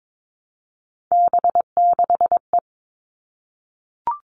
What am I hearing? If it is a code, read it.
Morse code, heard as a keyed tone, about 22 wpm, 700 Hz. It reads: B6E
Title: All Cut Numbers - 3-Digit 22wpm